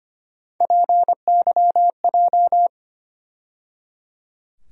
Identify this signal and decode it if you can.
Morse code, heard as a keyed tone, about 25 wpm, 700 Hz. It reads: PYJ
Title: USA Call Sign Postfixes - 3 Letters 25wpm